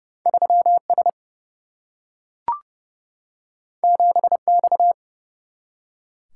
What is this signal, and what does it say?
Morse code, heard as a keyed tone, about 30 wpm, 700 Hz. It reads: 3S 7X
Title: Single-Letter Single-Number 30wpm